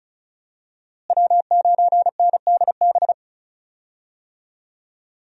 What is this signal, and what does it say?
Morse code, heard as a keyed tone, about 36 wpm, 700 Hz. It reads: W9NDB
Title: USA Call Signs - 5 Characters - Repeated 3x - 35wpm